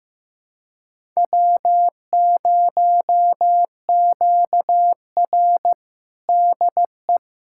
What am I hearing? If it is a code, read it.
Morse code, heard as a keyed tone, about 15 wpm, 700 Hz. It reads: W0QR DE
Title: USA Callsign DE Callsign - 4 Characters - 15wpm